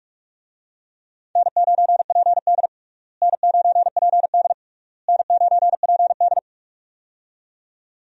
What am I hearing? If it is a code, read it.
Morse code, heard as a keyed tone, about 46 wpm, 700 Hz. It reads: N9PD N9PD N9PD
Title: USA Call Signs - 4 Characters - Repeated 3x - 45wpm